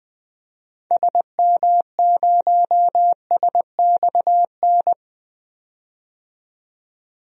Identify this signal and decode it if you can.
Morse code, heard as a keyed tone, about 20 wpm, 700 Hz. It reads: SM0SXN
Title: DXCC-100 Call Signs - 6 Characters - 20wpm